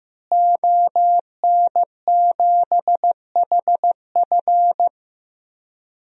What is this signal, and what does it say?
Morse code, heard as a keyed tone, about 16 wpm, 700 Hz. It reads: ON7HF
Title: DXCC-100 Call Signs - 5 Characters - Repeated 3x - 15wpm